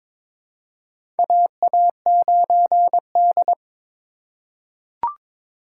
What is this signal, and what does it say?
Morse code, heard as a keyed tone, about 22 wpm, 700 Hz. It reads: AA9D
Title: USA Call Signs - 4 Characters - 6x Character Spacing - 22wpm